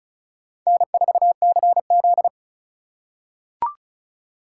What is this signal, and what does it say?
Morse code, heard as a keyed tone, about 36 wpm, 700 Hz. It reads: N4CZ
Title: USA Call Signs - 4 Characters - 3x Character Spacing - 35wpm